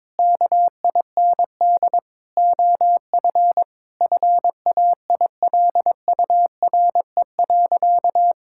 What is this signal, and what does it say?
Morse code, heard as a keyed tone, about 22 wpm, 700 Hz. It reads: KIND OF FAILURE.
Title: Sentences from Top 1000 Words - Review Part 6 - 22wpm